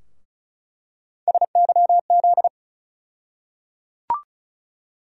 Morse code, heard as a keyed tone, about 36 wpm, 700 Hz. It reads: SYZ
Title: USA Call Sign Postfixes - 3 Letters 35wpm